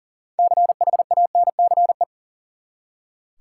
Morse code, heard as a keyed tone, about 40 wpm, 700 Hz. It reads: CHANCE
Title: Six Letter Words - 40wpm